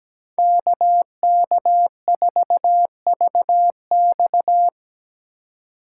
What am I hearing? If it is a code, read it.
Morse code, heard as a keyed tone, about 20 wpm, 700 Hz. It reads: KK4VX
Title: Speed Racing - USA 2x2 Call Signs 17wpm